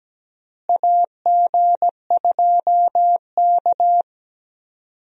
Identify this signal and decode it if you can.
Morse code, heard as a keyed tone, about 17 wpm, 700 Hz. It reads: AG2K
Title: USA Call Signs - 4 Characters - Repeated 3x - 17wpm